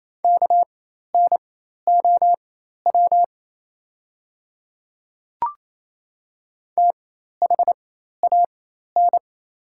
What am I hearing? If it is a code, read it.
Morse code, heard as a keyed tone, about 27 wpm, spaced as for 13 wpm, 700 Hz. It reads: KNOW THAN
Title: 4x Character Spacing - Top 100 Words - 28wpm